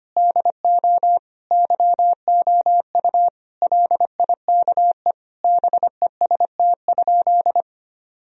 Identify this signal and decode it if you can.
Morse code, heard as a keyed tone, about 25 wpm, 700 Hz. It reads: DO YOU LIKE BEST?
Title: Sentences from Top 1000 Words - Review Part 1 - 25wpm